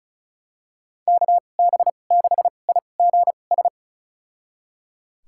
Morse code, heard as a keyed tone, about 36 wpm, spaced as for 25 wpm, 700 Hz. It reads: KB6IGS
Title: USA Call Signs - 2x Character Spacing - 35wpm